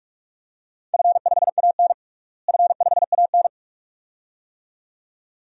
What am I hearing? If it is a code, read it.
Morse code, heard as a keyed tone, about 46 wpm, 700 Hz. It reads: F5AN F5AN
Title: DXCC-100 Call Signs - 4 Characters - Repeated 2x - 45wpm